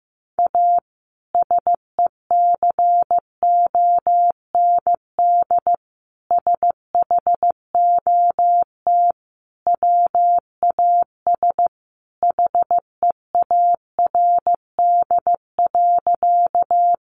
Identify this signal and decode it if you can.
Morse code, heard as a keyed tone, about 15 wpm, 700 Hz. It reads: A SECOND SHOT WAS HEARD.
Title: Sentences from Top 1100 Words 15wpm